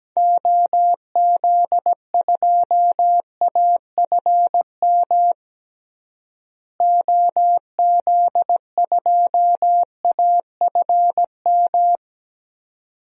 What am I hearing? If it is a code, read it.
Morse code, heard as a keyed tone, about 17 wpm, 700 Hz. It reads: OZ2AFM OZ2AFM
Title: DXCC-100 Call Signs - 6 Characters - Repeated 2x - 17wpm